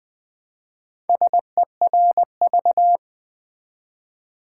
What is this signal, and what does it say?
Morse code, heard as a keyed tone, about 20 wpm, 700 Hz. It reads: SERV
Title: Build A Word - 1 Letter At A Time - 5-Letter Words - 20wpm